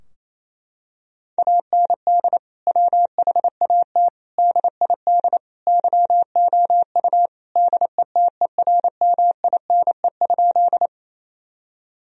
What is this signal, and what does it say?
Morse code, heard as a keyed tone, about 27 wpm, 700 Hz. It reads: AND WHAT DID YOU DETERMINE?
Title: Sentences from Top 1000 Words - Review Part 4 - 28wpm